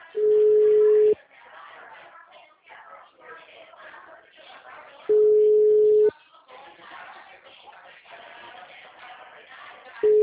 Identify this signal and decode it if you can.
Ringback tone (Europe, 425 Hz), with speech.
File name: out- -2021- - - .109